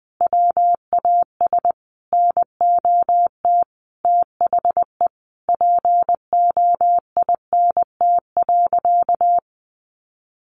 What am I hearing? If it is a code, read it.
Morse code, heard as a keyed tone, about 20 wpm, 700 Hz. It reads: WAS NOT THE POINT.
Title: Sentences from Top 600 Words 20wpm